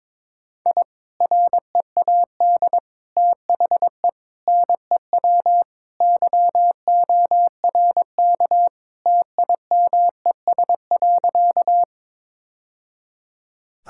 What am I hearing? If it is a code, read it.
Morse code, heard as a keyed tone, about 22 wpm, 700 Hz. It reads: I READ THE NEW YORK TIMES.
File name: practice / 22wpm / Sentences with 3-Word Phrases - 22wpm